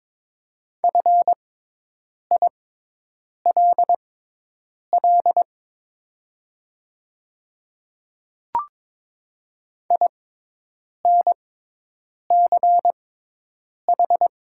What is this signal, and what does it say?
Morse code, heard as a keyed tone, about 22 wpm, spaced as for 8 wpm, 700 Hz. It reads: FILL INCH
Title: 6x Character Spacing - Top 500 Words - 22wpm